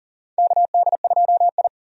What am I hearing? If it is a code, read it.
Morse code, heard as a keyed tone, about 37 wpm, 700 Hz. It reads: KD2I
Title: USA Call Signs - 4 Characters - Repeated 2x - 40wpm